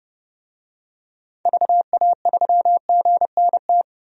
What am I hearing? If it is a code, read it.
Morse code, heard as a keyed tone, about 30 wpm, 700 Hz. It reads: VA3GNT
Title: DXCC-100 Call Signs - 6 Characters - 30wpm